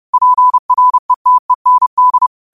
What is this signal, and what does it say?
Morse code, heard as a keyed tone, about 30 wpm, 1000 Hz. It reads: PRETEND
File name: qotdfiles / 30wpm